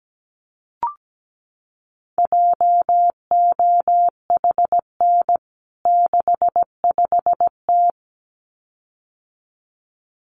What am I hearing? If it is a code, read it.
Morse code, heard as a keyed tone, about 17 wpm, 700 Hz. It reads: JOHN 65T
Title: CWT Exchange 17wpm